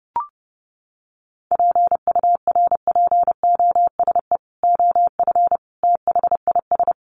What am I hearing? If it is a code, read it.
Morse code, heard as a keyed tone, about 30 wpm, 700 Hz. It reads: PURPOSE OF THIS
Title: Sets of 5 Words 30wpm